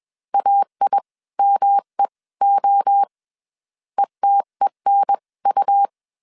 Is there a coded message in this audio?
Yes: Morse code from 0.3 to 5.9 s, heard as a keyed tone, about 21 wpm, spaced as for 18 wpm, 800 Hz. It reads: AIMEO ETENU